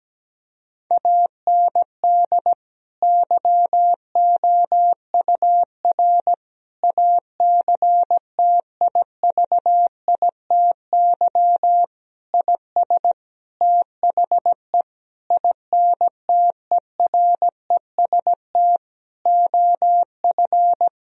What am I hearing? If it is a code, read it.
Morse code, heard as a keyed tone, about 17 wpm, 700 Hz. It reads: AND YOUR ACTIVITY IS THE INTEREST OF
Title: Sentences from Top 1000 Words - Review Part 4 - 17wpm